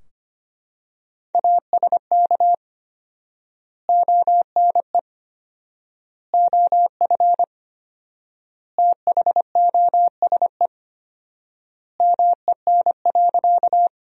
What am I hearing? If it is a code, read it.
Morse code, heard as a keyed tone, about 25 wpm, 700 Hz. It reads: ASK ONE OF THOSE MEN.
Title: 4x Word-Spacing - Sentences from Top 300 Words - 25wpm